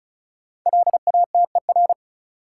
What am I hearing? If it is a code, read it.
Morse code, heard as a keyed tone, about 36 wpm, 700 Hz. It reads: LATER
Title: Sentences from Top 700 Words - New Word Drill - 35wpm